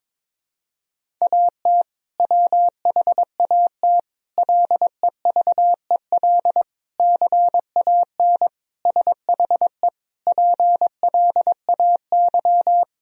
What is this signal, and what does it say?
Morse code, heard as a keyed tone, about 23 wpm, 700 Hz. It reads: AT WHAT LEVEL CAN SHE PLAY
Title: Sentences from Top 1000 Words - Review Part 4 - 22wpm